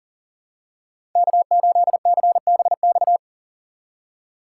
Morse code, heard as a keyed tone, about 37 wpm, 700 Hz. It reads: K8CBX
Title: USA Call Signs - 5 Characters - 6x Character Spacing - 40wpm